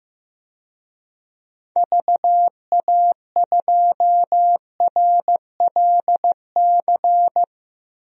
Morse code, heard as a keyed tone, about 15 wpm, 700 Hz. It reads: VA2RLC
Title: DXCC-100 Call Signs - 6 Characters - 15wpm